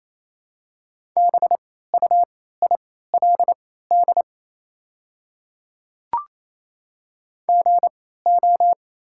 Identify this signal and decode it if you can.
Morse code, heard as a keyed tone, about 28 wpm, spaced as for 16 wpm, 700 Hz. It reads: BUILD GO
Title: 3x Character Spacing - Top 200 Words - 28wpm